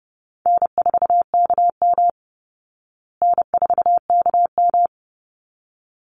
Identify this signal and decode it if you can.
Morse code, heard as a keyed tone, about 30 wpm, 700 Hz. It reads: N4KM N4KM
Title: Easy Call Signs Repeated 2x - 30wpm